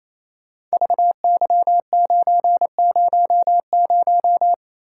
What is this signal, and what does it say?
Morse code, heard as a keyed tone, about 28 wpm, 700 Hz. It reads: VY900
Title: Code Groups - 5 Letter:Numbers - 28wpm